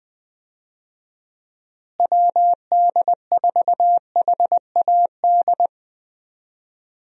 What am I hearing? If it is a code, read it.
Morse code, heard as a keyed tone, about 20 wpm, 700 Hz. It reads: WD4HAD
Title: USA Call Signs - 6 Characters - Repeated 3x - 20wpm